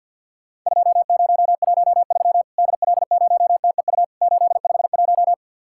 Morse code, heard as a keyed tone, about 50 wpm, 700 Hz. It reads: 2013 DL0TEU 851
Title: Worked All Europe DX Contest - QTC Exchange with Repeats - 50wpm